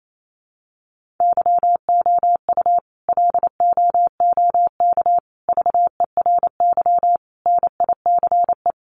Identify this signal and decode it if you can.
Morse code, heard as a keyed tone, about 28 wpm, 700 Hz. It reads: YOU LOOK VERY NICE
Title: Sentences from Top 1500 Words 28wpm